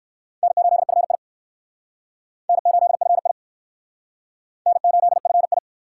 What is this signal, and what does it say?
Morse code, heard as a keyed tone, about 55 wpm, 700 Hz. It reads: N8FI N8FI N8FI
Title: Speed Racing - USA 1x2 Call Signs 45wpm